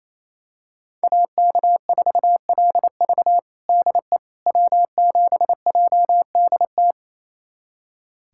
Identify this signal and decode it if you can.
Morse code, heard as a keyed tone, about 28 wpm, 700 Hz. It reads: AK4LV DE W7JDT
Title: USA Callsign DE Callsign - 5 Characters - 28wpm